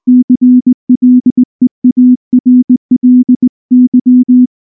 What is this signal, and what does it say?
Morse code, heard as a keyed tone, about 19 wpm, 250 Hz. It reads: CLEARLY
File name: morse / morse-training-20WPM-2